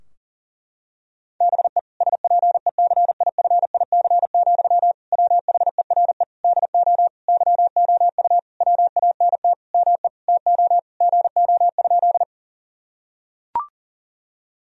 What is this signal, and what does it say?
Morse code, heard as a keyed tone, about 37 wpm, 700 Hz. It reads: BE SPECIFIC, WHERE DO YOU WANT ME TO GO?
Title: Sentences from Top 1000 Words - Review Part 3 - 40wpm